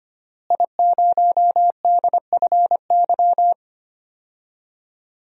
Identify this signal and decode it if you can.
Morse code, heard as a keyed tone, about 25 wpm, 700 Hz. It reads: I0DFY
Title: DXCC-100 Call Signs - 5 Characters - Repeated 3x - 25wpm